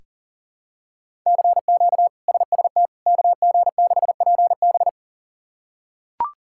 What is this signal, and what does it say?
Morse code, heard as a keyed tone, about 40 wpm, 700 Hz. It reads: CQ SST KG6PB
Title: K1USN SST QSO Practice - 40wpm